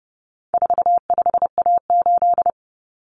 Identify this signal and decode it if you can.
Morse code, heard as a keyed tone, about 30 wpm, 700 Hz. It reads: 45A8
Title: Common Cut Numbers - 4-Digit 30wpm